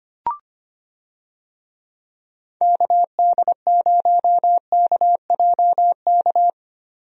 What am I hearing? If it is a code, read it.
Morse code, heard as a keyed tone, about 25 wpm, 700 Hz. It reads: KD0KJK
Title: USA Call Signs - 6 Characters - Repeated 3x - 25wpm